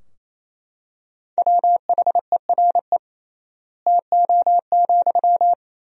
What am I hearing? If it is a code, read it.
Morse code, heard as a keyed tone, about 28 wpm, 700 Hz. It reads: WHERE TO,
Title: 3x Word-Spacing - Sentences from Top 200 Words - 28wpm